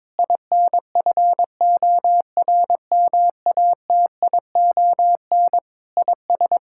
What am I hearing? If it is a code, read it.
Morse code, heard as a keyed tone, about 22 wpm, 700 Hz. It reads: INFORMATION IS
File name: practice / 22wpm / Sets of 6 Words - 22wpm